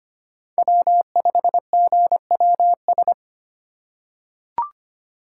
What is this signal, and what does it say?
Morse code, heard as a keyed tone, about 25 wpm, 700 Hz. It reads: W5GWS
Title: USA Call Signs - 5 Characters - 3x Character Spacing - 25wpm